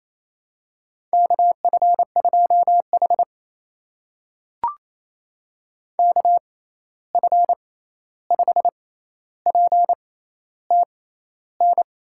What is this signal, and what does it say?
Morse code, heard as a keyed tone, about 28 wpm, spaced as for 13 wpm, 700 Hz. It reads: KF2H KF5PTN
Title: USA Call Signs - 6x Character Spacing - 28wpm